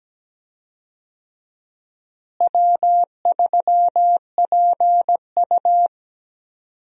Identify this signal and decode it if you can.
Morse code, heard as a keyed tone, about 17 wpm, 700 Hz. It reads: W3PU
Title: USA Call Signs - 4 Characters - Repeated 2x - 17wpm